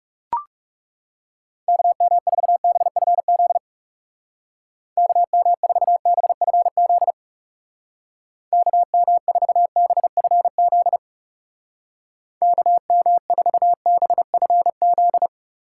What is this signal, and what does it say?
Morse code, heard as a keyed tone, about 36 wpm, 700 Hz. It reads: KM4BFZ KM4BFZ KM4BFZ KM4BFZ
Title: Speed Racing - USA Call Signs 30wpm